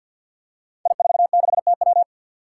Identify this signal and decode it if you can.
Morse code, heard as a keyed tone, about 52 wpm, 700 Hz. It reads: I46TW
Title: Five Challenging Letter-Number Combinations 50wpm